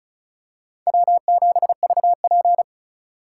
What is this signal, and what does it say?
Morse code, heard as a keyed tone, about 36 wpm, 700 Hz. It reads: W7VP
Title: K1USN SST QSO Practice - 35wpm